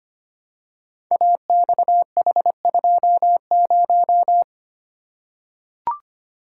Morse code, heard as a keyed tone, about 25 wpm, 700 Hz. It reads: AXH20
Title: Code Groups - 5 Letter:Numbers - 25wpm